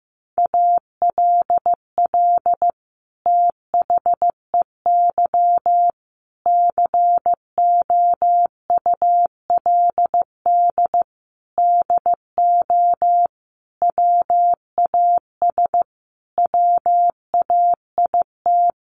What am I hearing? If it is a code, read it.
Morse code, heard as a keyed tone, about 15 wpm, 700 Hz. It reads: ALL THEY COULD DO WAS WAIT
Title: Sentences from Top 1300 Words 15wpm